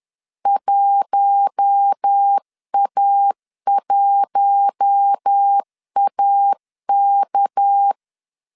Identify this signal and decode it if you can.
Morse code, heard as a keyed tone, about 10 wpm, 800 Hz. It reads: 1A1AK
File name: CW / DET / 1 6 1500C S C 10wpm 4252